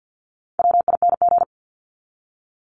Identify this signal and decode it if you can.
Morse code, heard as a keyed tone, about 50 wpm, 700 Hz. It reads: WING
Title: Top 1001-1250 Words 50wpm